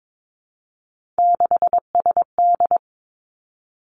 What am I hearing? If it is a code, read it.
Morse code, heard as a keyed tone, about 22 wpm, 700 Hz. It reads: 6SD
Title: Three Challenging Letter-Number Combinations 22wpm